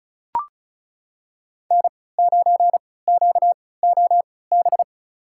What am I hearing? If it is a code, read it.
Morse code, heard as a keyed tone, about 36 wpm, spaced as for 20 wpm, 700 Hz. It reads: N9QOB
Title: USA Call Signs - 5 Characters - 3x Character Spacing - 35wpm